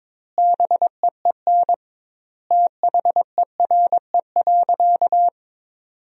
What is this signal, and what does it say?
Morse code, heard as a keyed tone, about 22 wpm, 700 Hz. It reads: BEEN THERE.
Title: 2x Word-Spacing - Sentences from Top 300 Words - 22wpm